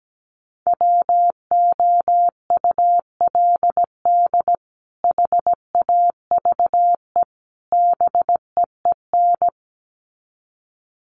Morse code, heard as a keyed tone, about 17 wpm, 700 Hz. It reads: WOULD HAVE BEEN
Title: Sets of 3 Words 17wpm